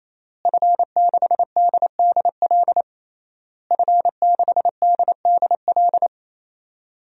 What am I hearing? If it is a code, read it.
Morse code, heard as a keyed tone, about 28 wpm, 700 Hz. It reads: F6DDL F6DDL
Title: DXCC-100 Call Signs - 5 Characters - Repeated 3x - 28wpm